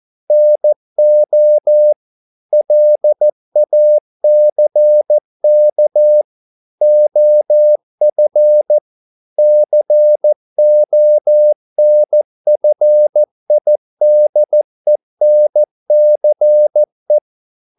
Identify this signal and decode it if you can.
Morse code, heard as a keyed tone, about 14 wpm, 600 Hz. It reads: NO LACK OF CONFIDENCE